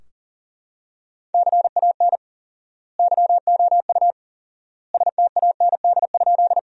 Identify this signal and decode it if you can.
Morse code, heard as a keyed tone, about 40 wpm, 700 Hz. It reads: CAN YOU STAND?
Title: 4x Word-Spacing - Sentences from Top 300 Words - 40wpm